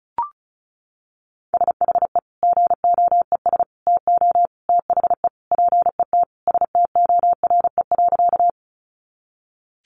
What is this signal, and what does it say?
Morse code, heard as a keyed tone, about 34 wpm, 700 Hz. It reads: SHE GOES TO THE PET STORE.
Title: Easy Phrases 35wpm